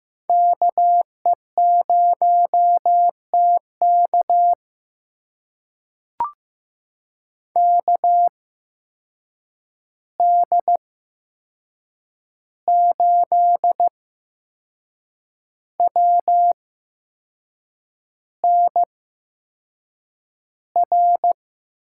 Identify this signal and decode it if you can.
Morse code, heard as a keyed tone, about 15 wpm, spaced as for 6 wpm, 700 Hz. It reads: KE0TK KD8WNR
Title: USA Call Signs - 8x Character Spacing - 15wpm